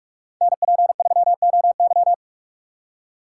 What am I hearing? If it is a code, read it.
Morse code, heard as a keyed tone, about 42 wpm, 700 Hz. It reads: NP3OY
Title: USA Call Signs - 6x Character Spacing - 45wpm